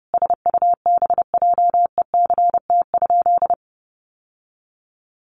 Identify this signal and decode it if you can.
Morse code, heard as a keyed tone, about 30 wpm, 700 Hz. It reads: SUBJECT?
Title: Sentences from Top 800 Words 30wpm